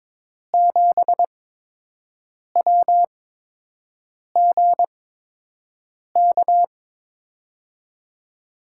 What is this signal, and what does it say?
Morse code, heard as a keyed tone, about 21 wpm, spaced as for 6 wpm, 700 Hz. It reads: 7WGK
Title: USA Call Signs - 5 Characters - 8x Character Spacing - 22wpm